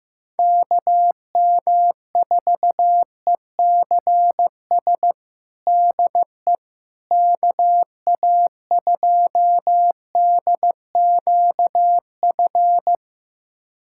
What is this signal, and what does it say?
Morse code, heard as a keyed tone, about 15 wpm, 700 Hz. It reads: KM4ECS DE KA2DQF
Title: USA Callsign DE Callsign - 6 Characters - 15wpm